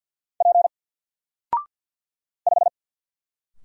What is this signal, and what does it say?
Morse code, heard as a keyed tone, about 50 wpm, 700 Hz. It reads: P 5
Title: Single Letter-Number 50wpm